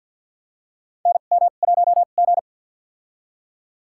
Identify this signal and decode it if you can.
Morse code, heard as a keyed tone, about 52 wpm, spaced as for 36 wpm, 700 Hz. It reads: NM1G
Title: USA Call Signs - 4 Characters - 2x Character Spacing - 50wpm